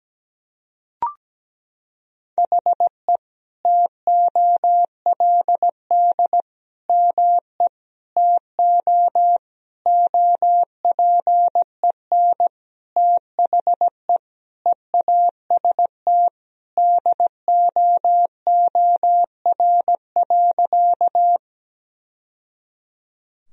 Morse code, heard as a keyed tone, about 17 wpm, 700 Hz. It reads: HE TOLD ME TO OPEN THE EAST DOOR.
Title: Sentences from Top 1000 Words - Review Part 3 - 17wpm